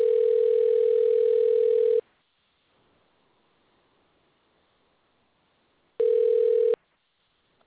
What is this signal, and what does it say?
Ringback tone (North America), 440 + 480 Hz.
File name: recordings / REd c a3fe982c